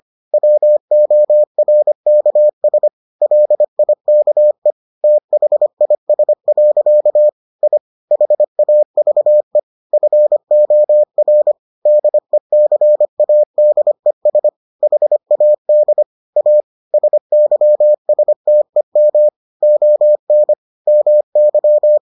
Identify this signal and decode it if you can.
Morse code, heard as a keyed tone, about 25 wpm, 600 Hz. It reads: WORKS LIKE THIS. I HAVE FOR DECADES HAD A SYSTEM ON MY